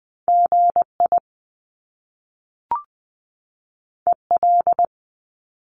Morse code, heard as a keyed tone, about 20 wpm, 700 Hz. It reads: GI EL
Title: Top Two Letter Combinations in Words 20wpm